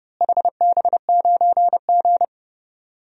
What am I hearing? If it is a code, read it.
Morse code, heard as a keyed tone, about 30 wpm, 700 Hz. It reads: HB9G
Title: DXCC-100 Call Signs - 4 Characters - Repeated 3x - 30wpm